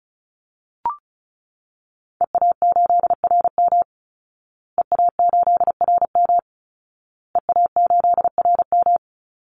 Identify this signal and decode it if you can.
Morse code, heard as a keyed tone, about 34 wpm, 700 Hz. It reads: EA8RM EA8RM EA8RM
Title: Call Signs Repeated 2x - 35wpm